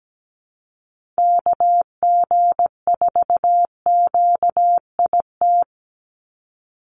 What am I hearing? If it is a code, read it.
Morse code, heard as a keyed tone, about 17 wpm, 700 Hz. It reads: KG4QIT
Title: USA Call Signs 17wpm